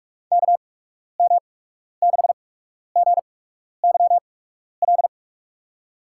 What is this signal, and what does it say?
Morse code, heard as a keyed tone, about 46 wpm, spaced as for 12 wpm, 700 Hz. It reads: KM6GYL
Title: USA Call Signs - 6 Characters - 8x Character Spacing - 45wpm